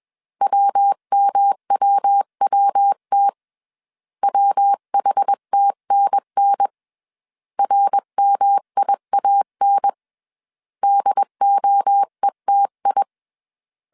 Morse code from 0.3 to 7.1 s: WMWWT WHTNN